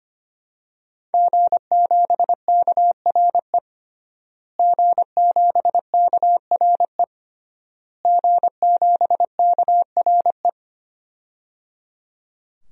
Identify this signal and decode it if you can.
Morse code, heard as a keyed tone, about 25 wpm, 700 Hz. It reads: G7KRE G7KRE G7KRE
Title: DXCC-100 Call Signs - 5 Characters - Repeated 3x - 25wpm